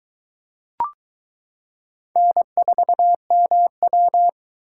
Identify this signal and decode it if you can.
Morse code, heard as a keyed tone, about 22 wpm, 700 Hz. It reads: N4MW